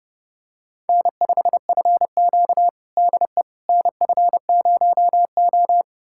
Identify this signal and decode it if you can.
Morse code, heard as a keyed tone, about 32 wpm, 700 Hz. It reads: N5FQ DE NF0O
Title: USA Callsign DE Callsign - 4 Characters - 30wpm